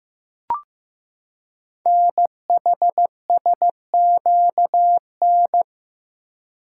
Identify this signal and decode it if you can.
Morse code, heard as a keyed tone, about 15 wpm, 700 Hz. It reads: NHSQN
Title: Code Groups - 5 Letter:Numbers - 15wpm